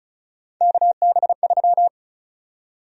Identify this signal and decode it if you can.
Morse code, heard as a keyed tone, about 36 wpm, 700 Hz. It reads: KB3
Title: USA Call Sign Prefixes - 35wpm